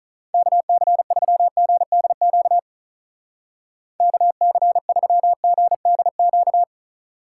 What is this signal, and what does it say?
Morse code, heard as a keyed tone, about 46 wpm, 700 Hz. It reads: KC3GDQ KC3GDQ